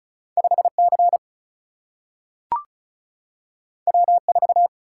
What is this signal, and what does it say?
Morse code, heard as a keyed tone, about 36 wpm, 700 Hz. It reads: 5 C W 4
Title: Single-Letter Single-Number 35wpm